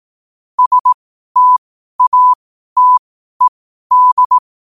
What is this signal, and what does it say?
Morse code, heard as a keyed tone, about 18 wpm, spaced as for 12 wpm, 1000 Hz. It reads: STATED
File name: qotdfiles / 12wpm